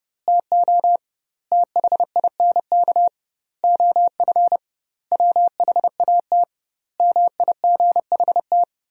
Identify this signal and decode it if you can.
Morse code, heard as a keyed tone, about 30 wpm, 700 Hz. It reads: TO THINK OF WHAT MIGHT
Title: 2x Word-Spacing - Sentences from Top 400 Words - 30wpm